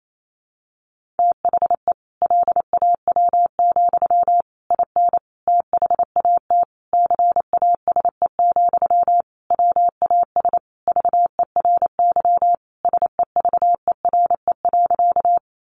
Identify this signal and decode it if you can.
Morse code, heard as a keyed tone, about 28 wpm, 700 Hz. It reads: THE LAW, IN THAT CASE, WAS VERY SEVERE.
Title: Sentences from Top 1400 Words 28wpm